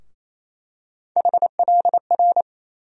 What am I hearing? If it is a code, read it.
Morse code, heard as a keyed tone, about 28 wpm, 700 Hz. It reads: HLR
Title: USA Call Sign Postfixes - 3 Letters 28wpm